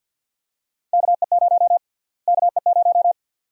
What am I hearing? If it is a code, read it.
Morse code, heard as a keyed tone, about 53 wpm, 700 Hz. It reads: KE0 KE0
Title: USA Call Sign Prefixes - Repeated 2x - 50wpm